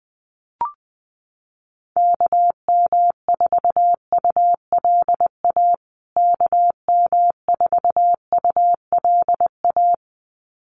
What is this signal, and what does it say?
Morse code, heard as a keyed tone, about 20 wpm, 700 Hz. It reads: KM4ULA KM4ULA
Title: USA Call Signs Repeated 2x 20wpm